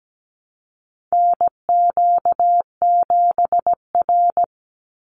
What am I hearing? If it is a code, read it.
Morse code, heard as a keyed tone, about 17 wpm, 700 Hz. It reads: NQ7R
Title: NA SOTA Chaser Callsigns 17wpm